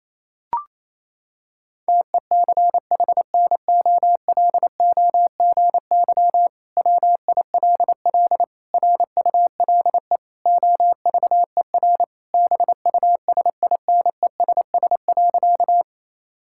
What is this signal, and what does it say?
Morse code, heard as a keyed tone, about 28 wpm, 700 Hz. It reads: TECHNOLOGY WILL RULE OVER BUSINESS.
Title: Sentences from Top 1000 Words - Review Part 5 - 28wpm